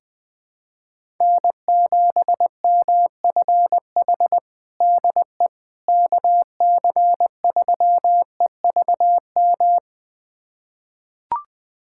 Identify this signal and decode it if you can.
Morse code, heard as a keyed tone, about 20 wpm, 700 Hz. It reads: N7MFH DE KC3EVM
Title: USA Callsign DE Callsign 20wpm